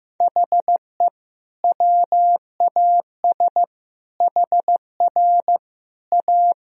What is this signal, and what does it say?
Morse code, heard as a keyed tone, about 15 wpm, 700 Hz. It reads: HE WAS HR A
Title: Sentences from Top 500 Words and Abbreviations 15wpm